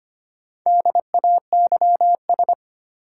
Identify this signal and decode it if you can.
Morse code, heard as a keyed tone, about 25 wpm, 700 Hz. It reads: DAYS
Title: Sets of 5 Words Encore - New Word Drill - 25wpm